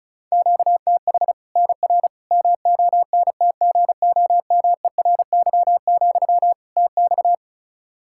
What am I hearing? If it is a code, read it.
Morse code, heard as a keyed tone, about 36 wpm, 700 Hz. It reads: QTH NR MONTGOMERY, TX
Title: Practice QSO - Exchange 1 - Beginning - 35wpm